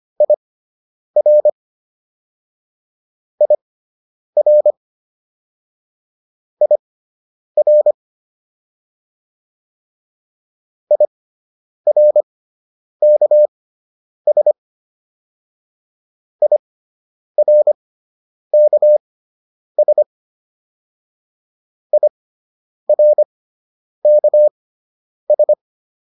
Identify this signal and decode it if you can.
Morse code, heard as a keyed tone, about 25 wpm, spaced as for 9 wpm, 600 Hz. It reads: IR IR IR IRKS IRKS IRKS